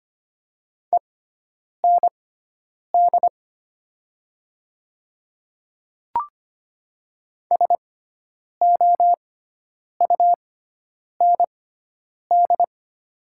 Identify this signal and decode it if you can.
Morse code, heard as a keyed tone, about 25 wpm, spaced as for 9 wpm, 700 Hz. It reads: END SOUND
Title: 6x Character Spacing - Top 200 Words - 25wpm